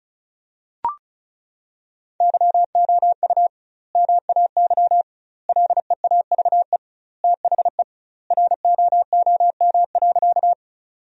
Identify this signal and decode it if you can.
Morse code, heard as a keyed tone, about 34 wpm, 700 Hz. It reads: YOU MAY LEAVE THE ROOM.
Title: 2x Word-Spacing - Sentences from Top 400 Words - 35wpm